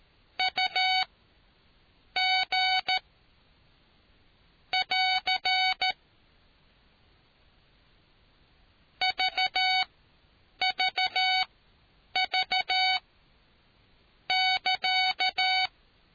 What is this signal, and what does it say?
Morse code, heard as a keyed tone, about 13 wpm, spaced as for 7 wpm, 750 Hz. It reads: UG <AR> VVV <KA>